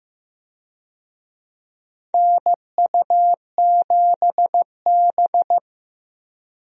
Morse code, heard as a keyed tone, about 16 wpm, 700 Hz. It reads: NU7B
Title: USA Call Signs - 4 Characters - Repeated 3x - 15wpm